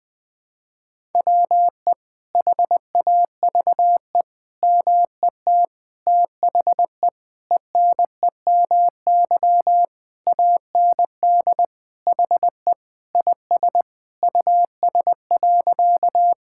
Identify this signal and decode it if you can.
Morse code, heard as a keyed tone, about 20 wpm, 700 Hz. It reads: WE HAVE MET THE ENEMY AND HE IS US.
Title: Sentences with 9-Word Phrases - Spoken Beginning - 20wpm